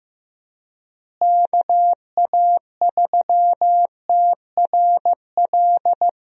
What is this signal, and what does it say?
Morse code, heard as a keyed tone, about 15 wpm, 700 Hz. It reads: KA3TRL
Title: USA Call Signs - 6 Characters - Repeated 3x - 15wpm